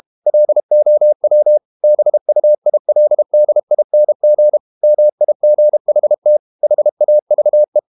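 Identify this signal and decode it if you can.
Morse code, heard as a keyed tone, about 32 wpm, 600 Hz. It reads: LOW BUILDING MIGHT HAVE
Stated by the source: Morse code practice at 32 wpm from CW med Gnister, reading Ken Follett's novel Never in English.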